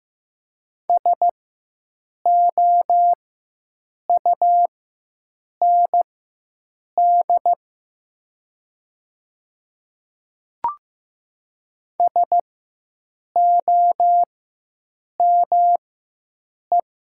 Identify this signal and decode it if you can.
Morse code, heard as a keyed tone, about 15 wpm, spaced as for 7 wpm, 700 Hz. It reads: SOUND SOME
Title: 4x Character Spacing - Top 100 Words - 15wpm